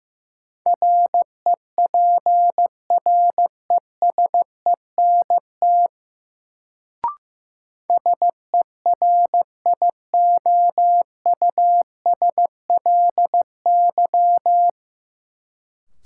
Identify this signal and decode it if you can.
Morse code, heard as a keyed tone, about 15 wpm, 700 Hz. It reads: REPRESENT SERIOUSLY
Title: Sentences from Top 1200 Words - New Word Drill - 15wpm